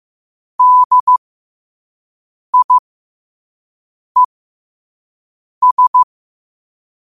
Morse code, heard as a keyed tone, about 15 wpm, spaced as for 5 wpm, 1000 Hz. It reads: DIES
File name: qotdfiles / 5wpm